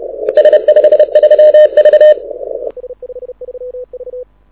Morse code, heard as a keyed tone, about 30 wpm, 610 Hz. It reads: S53V
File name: audio zawody / IARU 2015 CW SN7Q